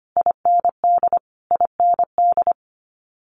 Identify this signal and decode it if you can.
Morse code, heard as a keyed tone, about 25 wpm, 700 Hz. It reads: IND IND
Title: Multi-letter Phonemes Repeated 3x 25wpm